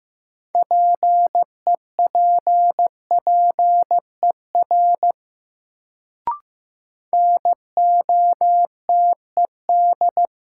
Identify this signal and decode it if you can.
Morse code, heard as a keyed tone, about 15 wpm, 700 Hz. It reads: PEPPER NOTED
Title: Sets of 5 Words - New Word Drill - 15wpm